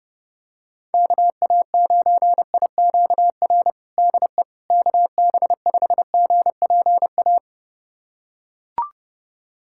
Morse code, heard as a keyed tone, about 32 wpm, 700 Hz. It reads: KA9IQR DE KB5GPA
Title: USA Callsign DE Callsign - 6 Characters - 30wpm